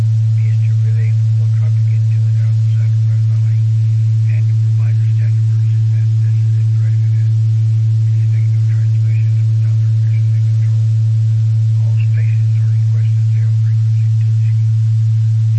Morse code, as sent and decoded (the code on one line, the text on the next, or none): none